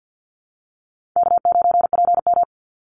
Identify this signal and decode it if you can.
Morse code, heard as a keyed tone, about 50 wpm, 700 Hz. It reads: K9PM
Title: NA SOTA Activator Callsigns 50wpm